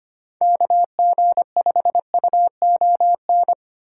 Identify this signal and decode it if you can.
Morse code, heard as a keyed tone, about 33 wpm, 700 Hz. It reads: KG5UON